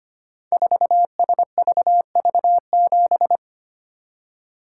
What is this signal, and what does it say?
Morse code, heard as a keyed tone, about 25 wpm, 700 Hz. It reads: 4SVV7
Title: Code Groups - 5 Letter:Numbers - 25wpm